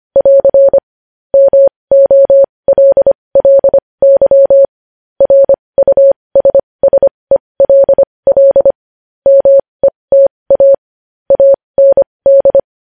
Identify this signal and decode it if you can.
Morse code, heard as a keyed tone, about 25 wpm, spaced as for 20 wpm, 550 Hz. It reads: <AR> MOLLY RUSSELL META AND
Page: News headlines in morse code at 20 wpm